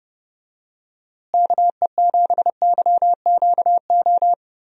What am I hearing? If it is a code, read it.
Morse code, heard as a keyed tone, about 30 wpm, 700 Hz. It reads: KE7YQO
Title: USA Call Signs - 6 Characters - Repeated 2x - 30wpm